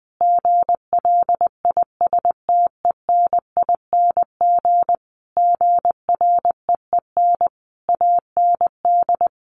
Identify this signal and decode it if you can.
Morse code, heard as a keyed tone, about 20 wpm, 700 Hz. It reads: GLISTENING GREEN AND
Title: A Sound Of Thunder 20wpm